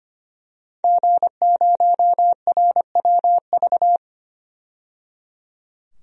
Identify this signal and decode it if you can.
Morse code, heard as a keyed tone, about 25 wpm, 700 Hz. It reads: G0RWV
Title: DXCC-100 Call Signs - 5 Characters - 25wpm